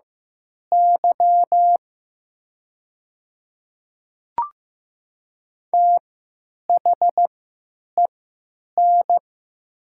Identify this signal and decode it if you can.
Morse code, heard as a keyed tone, about 15 wpm, spaced as for 9 wpm, 700 Hz. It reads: Y THEN
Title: 3x Character Spacing - Top 500 Words - 15wpm